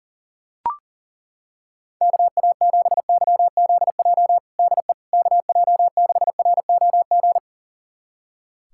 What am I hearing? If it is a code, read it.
Morse code, heard as a keyed tone, about 40 wpm, 700 Hz. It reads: KA7YZJ DE KJ6ROG
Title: USA Callsign DE Callsign - 6 Characters - 40wpm